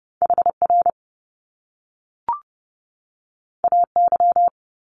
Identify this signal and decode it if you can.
Morse code, heard as a keyed tone, about 30 wpm, 700 Hz. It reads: HR AY
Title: Top Two Letter Combinations in Words 30wpm